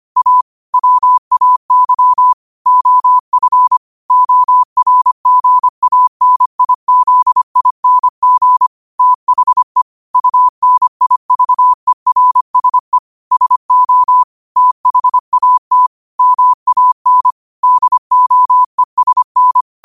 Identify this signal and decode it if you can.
Morse code, heard as a keyed tone, about 25 wpm, 1000 Hz. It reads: A WAY OF ORGANIZING THE UNIVERSE SO THAT MAN DOESN
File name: qotdfiles / 25wpm